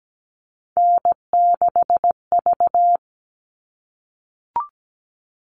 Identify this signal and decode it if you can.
Morse code, heard as a keyed tone, about 17 wpm, 700 Hz. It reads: N6V
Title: All Cut Numbers - 3-Digit 17wpm